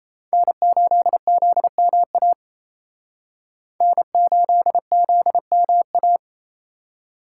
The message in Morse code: N8ZMA N8ZMA

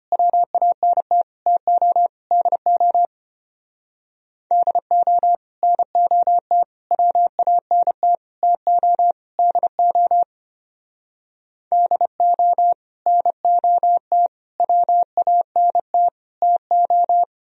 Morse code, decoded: WANT TO DO DO NOT WANT TO DO DO NOT WANT TO

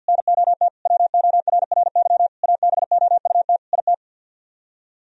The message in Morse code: NOT WORRY ABOUT IT

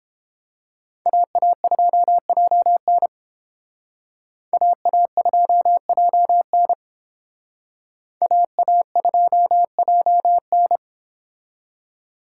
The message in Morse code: AA2JN AA2JN AA2JN

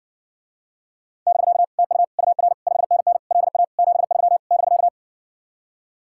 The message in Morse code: <BK> TU UR 5NN LA 73 <BK>

